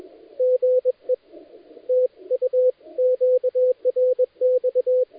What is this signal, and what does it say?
Morse code, heard as a keyed tone, about 21 wpm, 510 Hz. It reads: GE TU QRX
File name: dlazaawansowanych / nagraniazpasma / titawaNETcontrol